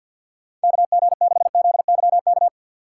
Morse code, heard as a keyed tone, about 51 wpm, 700 Hz. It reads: KG6ZYK
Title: USA Call Signs - 3x Character Spacing - 50wpm